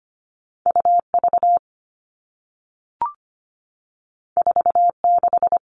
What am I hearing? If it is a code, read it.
Morse code, heard as a keyed tone, about 25 wpm, 700 Hz. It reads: UV 46
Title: All Cut Numbers - 2-Digit 25wpm